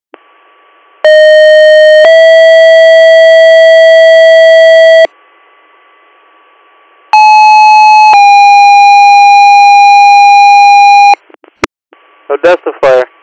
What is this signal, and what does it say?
Two-tone page, heard as a one-second tone, then a three-second tone, about 620 then 650 Hz, about 860 then 810 Hz, with speech. It is heard from 1.0 to 11.1 s.